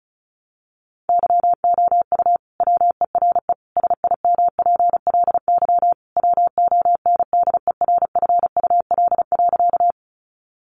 Morse code, heard as a keyed tone, about 34 wpm, 700 Hz. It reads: YOU WERE SIMPLY WONDERFUL.
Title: Sentences from Top 1100 Words 35wpm